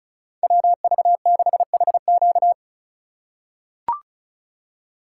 Morse code, heard as a keyed tone, about 34 wpm, 700 Hz. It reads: WV6HQ
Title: USA Call Signs - 5 Characters - 2x Character Spacing - 35wpm